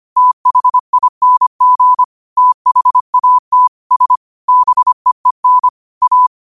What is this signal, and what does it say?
Morse code, heard as a keyed tone, about 25 wpm, 1000 Hz. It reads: THING THAT S BEEN A
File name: qotdfiles / 25wpm